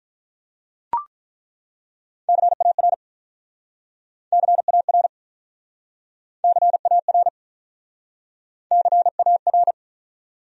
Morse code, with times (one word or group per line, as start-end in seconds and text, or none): 2.3-2.9 CAR
4.3-5.1 CAR
6.4-7.3 CAR
8.7-9.7 CAR